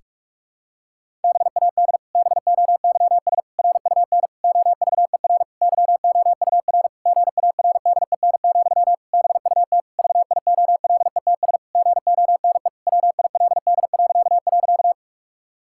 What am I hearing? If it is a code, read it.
Morse code, heard as a keyed tone, about 46 wpm, 700 Hz. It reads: BAD BOYS RUN OVER YOUR GARDEN, BUT VIOLETS GONE WILD..
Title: Pangram Challenge v2 - 45wpm